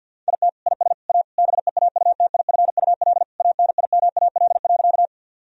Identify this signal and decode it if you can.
Morse code, heard as a keyed tone, about 51 wpm, 700 Hz. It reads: IT IS A BEAUTIFUL ANIMAL.
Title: Sentences from Top 1000 Words - Review Part 1 - 50wpm